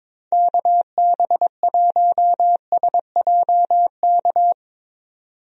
Morse code, heard as a keyed tone, about 22 wpm, 700 Hz. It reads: KB1SJK
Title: USA Call Signs - 6 Characters - 22wpm